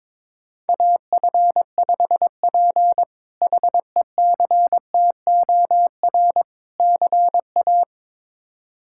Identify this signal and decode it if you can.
Morse code, heard as a keyed tone, about 22 wpm, 700 Hz. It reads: AF5P HECTOR CA
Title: K1USN SST QSO Practice - 22wpm